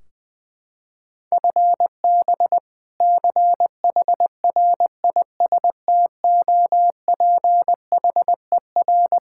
Morse code, heard as a keyed tone, about 20 wpm, 700 Hz. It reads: FB CHRISTOPHER
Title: K1USN SST QSO Practice 20wpm